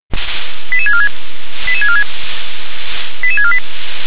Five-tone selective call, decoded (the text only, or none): 88646 88646 88648